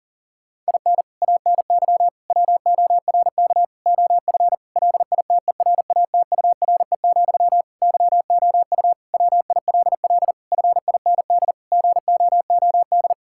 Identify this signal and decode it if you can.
Morse code, heard as a keyed tone, about 40 wpm, 700 Hz. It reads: IN ANY WORK OF LITERATURE, YOU WILL FIND GOOD
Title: Sentences from Top 1000 Words - Review Part 6 - 40wpm